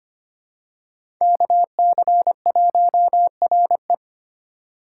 Morse code, heard as a keyed tone, about 25 wpm, 700 Hz. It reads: KC1RE
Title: USA Call Signs - 2x Character Spacing - 25wpm